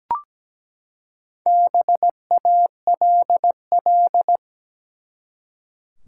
Morse code, heard as a keyed tone, about 17 wpm, 700 Hz. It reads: BALL
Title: Four Letter Words - 17wpm